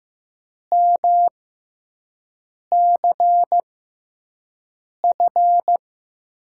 Morse code, heard as a keyed tone, about 15 wpm, spaced as for 5 wpm, 700 Hz. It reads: MCF
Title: USA Call Signs - 6 Characters - 6x Character Spacing - 15wpm